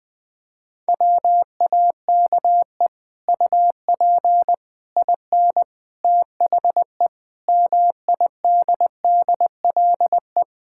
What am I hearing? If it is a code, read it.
Morse code, heard as a keyed tone, about 21 wpm, 700 Hz. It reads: WAKE UP IN THE MIDDLE
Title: Sets of 8 Words - 20wpm